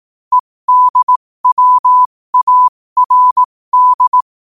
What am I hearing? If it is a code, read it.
Morse code, heard as a keyed tone, about 18 wpm, spaced as for 15 wpm, 1000 Hz. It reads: EDWARD
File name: qotdfiles / 15wpm